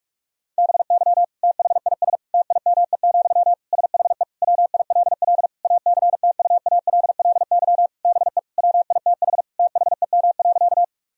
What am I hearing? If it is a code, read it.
Morse code, heard as a keyed tone, about 46 wpm, 700 Hz. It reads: BY THIS TIME, SHE WILL ACTUALLY BE WITH THEM.
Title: Sentences from Top 1000 Words - Review Part 4 - 45wpm